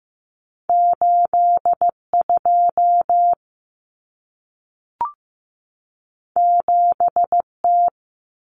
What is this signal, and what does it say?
Morse code, heard as a keyed tone, about 15 wpm, 700 Hz. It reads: 8 2 7 T
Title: Common Cut Numbers - 2-Digit 15wpm